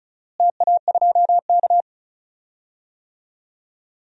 Morse code, heard as a keyed tone, about 36 wpm, 700 Hz. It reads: TA2K
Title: DXCC-100 Call Signs - 4 Characters - Repeated 2x - 35wpm